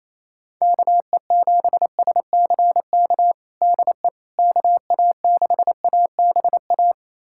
Morse code, heard as a keyed tone, about 27 wpm, 700 Hz. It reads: KE7SCK DE KA6ABA
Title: USA Callsign DE Callsign - 6 Characters - 28wpm